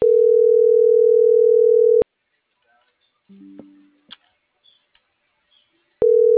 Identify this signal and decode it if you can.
Ringback tone (North America), 440 + 480 Hz.